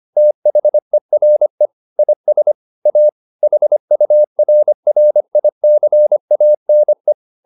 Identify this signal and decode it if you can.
Morse code, heard as a keyed tone, about 25 wpm, 600 Hz. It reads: THERE IS A HURRICANE
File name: the old man and the sea part 1 25wpm